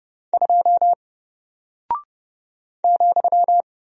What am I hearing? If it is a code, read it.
Morse code, heard as a keyed tone, about 30 wpm, 700 Hz. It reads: 2 ,
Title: Single Character (Letter:Number:Punctuation:Prosign) Alt 30wpm